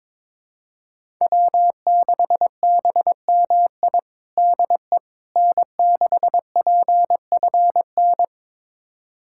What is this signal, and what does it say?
Morse code, heard as a keyed tone, about 22 wpm, 700 Hz. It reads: W6BMI DE N6PFN
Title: USA Callsign DE Callsign - 5 Characters - 22wpm